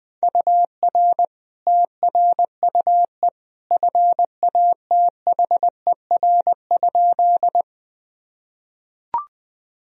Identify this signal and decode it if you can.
Morse code, heard as a keyed tone, about 21 wpm, 700 Hz. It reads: UR TRUE FATHER?
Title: Sentences from Top 400 Words and Abbreviations 20wpm